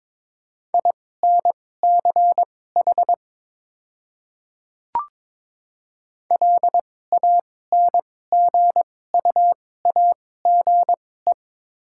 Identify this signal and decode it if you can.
Morse code, heard as a keyed tone, about 22 wpm, spaced as for 16 wpm, 690 Hz. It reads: INCH LANGUAGE
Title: 2x Character Spacing - Top 500 Words - 22wpm